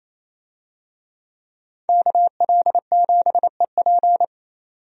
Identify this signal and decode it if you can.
Morse code, heard as a keyed tone, about 28 wpm, 700 Hz. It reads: KL7EP
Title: DXCC-100 Call Signs - 5 Characters - 28wpm